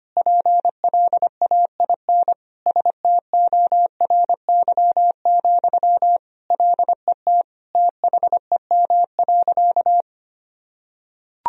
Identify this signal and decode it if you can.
Morse code, heard as a keyed tone, about 25 wpm, 700 Hz. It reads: PLAIN STORY, LET THEM.
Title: Sentences from Top 1000 Words - Review Part 2 - 25wpm